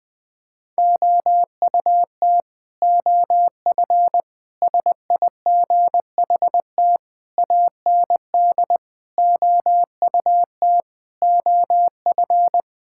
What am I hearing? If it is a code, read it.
Morse code, heard as a keyed tone, about 20 wpm, 700 Hz. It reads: OUT OF SIGHT AND OUT OF
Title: Sentences with 7-Word Phrases - 20wpm